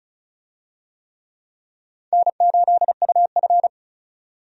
Morse code, heard as a keyed tone, about 36 wpm, 700 Hz. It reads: N8UF
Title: USA Call Signs - 4 Characters - 35wpm